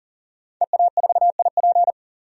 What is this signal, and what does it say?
Morse code, heard as a keyed tone, about 40 wpm, 700 Hz. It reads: EA4IP
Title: DXCC-100 Call Signs - 5 Characters - 40wpm